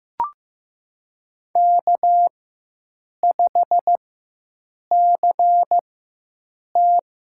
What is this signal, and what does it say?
Morse code, heard as a keyed tone, about 15 wpm, spaced as for 7 wpm, 700 Hz. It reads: K5CT
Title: USA Call Signs - 4 Characters - 4x Character Spacing - 15wpm